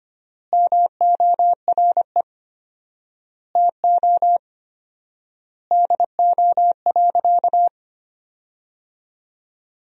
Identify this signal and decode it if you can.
Morse code, heard as a keyed tone, about 25 wpm, 700 Hz. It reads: MORE TO DO.
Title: 4x Word-Spacing - Sentences from Top 100 Words - 25wpm